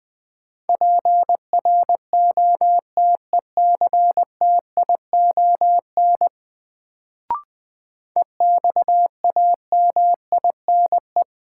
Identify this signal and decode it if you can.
Morse code, heard as a keyed tone, about 20 wpm, 700 Hz. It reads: PROTECTION EXAMINE
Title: Sentences from Top 1600 Words - New Word Drill - 20wpm